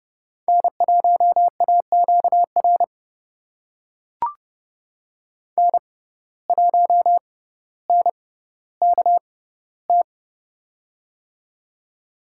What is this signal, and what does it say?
Morse code, heard as a keyed tone, about 30 wpm, spaced as for 15 wpm, 700 Hz. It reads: N1AQR N1NKT
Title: USA Call Signs - 5 Characters - 6x Character Spacing - 30wpm